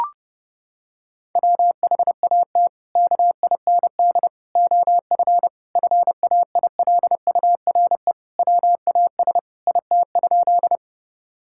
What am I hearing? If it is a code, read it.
Morse code, heard as a keyed tone, about 32 wpm, 700 Hz. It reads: WHAT KIND OF FAILURE WAS IT?
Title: Sentences from Top 1000 Words - Review Part 6 - 30wpm